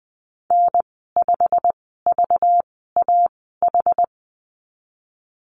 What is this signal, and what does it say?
Morse code, heard as a keyed tone, about 19 wpm, spaced as for 14 wpm, 700 Hz. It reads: N5VAH
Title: 193 Lesson 31g - Letter v - Call Signs 20wpm